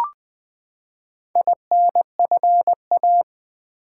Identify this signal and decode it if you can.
Morse code, heard as a keyed tone, about 20 wpm, 700 Hz. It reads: INFA
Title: Build A Word - 4 Letters At A Time - 8-Letter Words - 20wpm